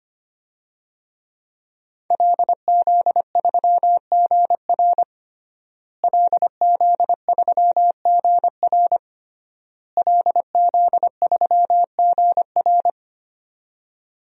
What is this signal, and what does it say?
Morse code, heard as a keyed tone, about 25 wpm, 700 Hz. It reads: LZ3GR LZ3GR LZ3GR
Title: DXCC-100 Call Signs - 5 Characters - Repeated 3x - 25wpm